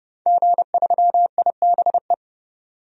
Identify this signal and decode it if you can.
Morse code, heard as a keyed tone, about 30 wpm, 700 Hz. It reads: G3IBE
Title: DXCC-100 Call Signs - 5 Characters - 30wpm